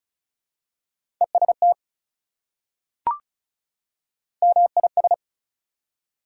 Morse code, heard as a keyed tone, about 36 wpm, 700 Hz. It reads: EST MIS
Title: Prefixes and Suffixes 35wpm